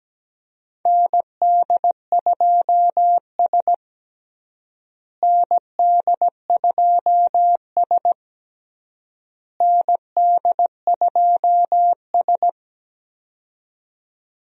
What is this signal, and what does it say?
Morse code, heard as a keyed tone, about 17 wpm, 700 Hz. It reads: ND2S ND2S ND2S
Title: USA Call Signs - 4 Characters - Repeated 3x - 17wpm